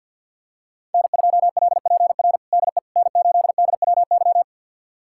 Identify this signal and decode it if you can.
Morse code, heard as a keyed tone, about 51 wpm, 700 Hz. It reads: N2LPR DE N8DWY
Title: USA Callsign DE Callsign - 5 Characters - 50wpm